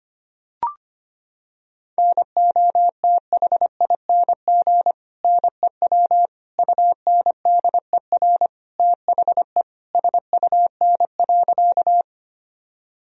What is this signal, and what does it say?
Morse code, heard as a keyed tone, about 25 wpm, 700 Hz. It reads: NOTHING NEW UNDER THE SUN.
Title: Sentences from Top 1000 Words - Review Part 1 - 25wpm